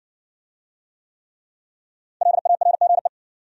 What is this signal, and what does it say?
Morse code, heard as a keyed tone, about 60 wpm, 700 Hz. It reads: LARGE